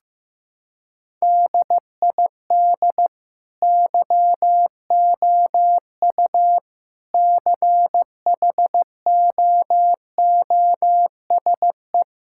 Morse code, heard as a keyed tone, about 15 wpm, 700 Hz. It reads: DID YOU CHOOSE
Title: Sentences from Top 1000 Words - Review Part 6 - 15wpm